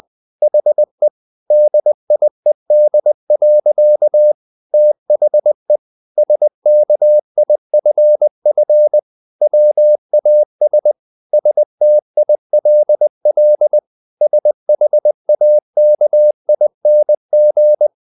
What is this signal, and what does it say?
Morse code, heard as a keyed tone, about 20 wpm, 600 Hz. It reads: HE DIED. THE SKIFF WAS STILL SHAKING
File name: the old man and the sea part 2 20wpm